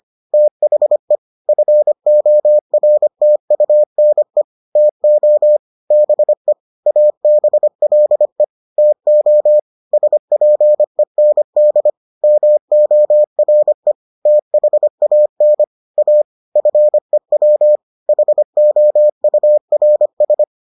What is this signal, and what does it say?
Morse code, heard as a keyed tone, about 25 wpm, 600 Hz. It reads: THE FORTUNE TO BE ABLE TO SPEND MORE THAN A FEW HOURS